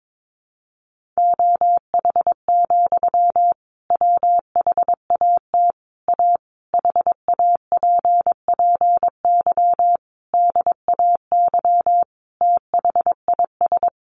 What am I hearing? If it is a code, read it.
Morse code, heard as a keyed tone, about 22 wpm, 700 Hz. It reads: OH, WHAT A HAPPY DAY THIS
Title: Sentences from Top 1500 Words 22wpm